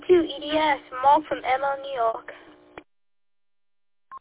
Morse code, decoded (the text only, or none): none